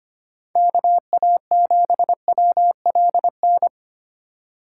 Morse code, heard as a keyed tone, about 25 wpm, 700 Hz. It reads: KA7WLN
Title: USA Call Signs - 6 Characters - 4x Character Spacing - 25wpm